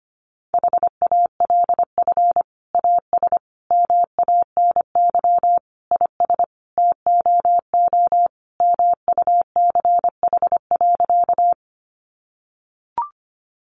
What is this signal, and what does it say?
Morse code, heard as a keyed tone, about 25 wpm, 700 Hz. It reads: HALF AS MANY IS TOO MUCH.
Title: Sentences from Top 400 Words 25wpm